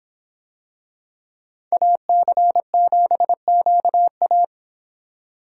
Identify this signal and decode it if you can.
Morse code, heard as a keyed tone, about 26 wpm, 700 Hz. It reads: AC7QA